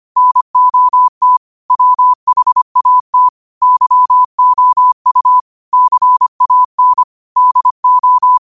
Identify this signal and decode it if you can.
Morse code, heard as a keyed tone, about 25 wpm, 1000 Hz. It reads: NOT WHAT YOU CAN DO
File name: qotdfiles / 25wpm